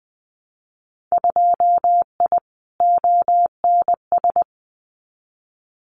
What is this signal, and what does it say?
Morse code, heard as a keyed tone, about 20 wpm, 700 Hz. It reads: 2I ONS
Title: Winter Field Day Exchanges Encore 20wpm